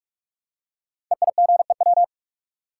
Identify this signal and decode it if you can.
Morse code, heard as a keyed tone, about 44 wpm, 700 Hz. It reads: EIGEW
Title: Code Groups - 5 Letters - 45wpm